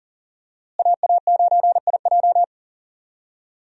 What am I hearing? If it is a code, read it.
Morse code, heard as a keyed tone, about 40 wpm, 700 Hz. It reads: AA9IJ
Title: USA Call Signs - 5 Characters - Repeated 2x - 40wpm